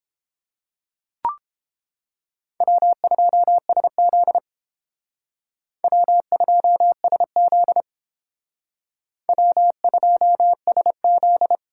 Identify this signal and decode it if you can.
Morse code, heard as a keyed tone, about 28 wpm, 700 Hz. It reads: W2SZ W2SZ W2SZ